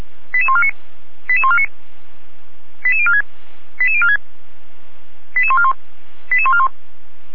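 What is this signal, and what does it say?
Five-tone selective call (ZVEI), heard as quick runs of five tones, tones of about 70 ms. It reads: 88149 88149 88046 88046 88141 88141
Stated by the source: Received on SDR, from feuerwehr-orlamuende.org, the Orlamünde fire brigade.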